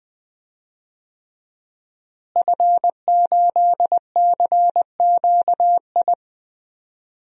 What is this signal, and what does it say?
Morse code, heard as a keyed tone, about 20 wpm, 700 Hz. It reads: F8CQI
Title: DXCC-100 Call Signs - 5 Characters - Repeated 3x - 20wpm